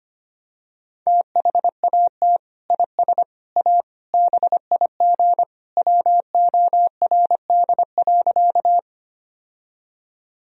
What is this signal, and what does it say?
Morse code, heard as a keyed tone, about 25 wpm, 700 Hz. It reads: THAT IS A BIG WORD.
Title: Sentences from Top 1000 Words - Review Part 1 - 25wpm